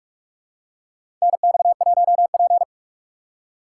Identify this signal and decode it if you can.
Morse code, heard as a keyed tone, about 42 wpm, 700 Hz. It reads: NX1P
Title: USA Call Signs - 4 Characters - 6x Character Spacing - 45wpm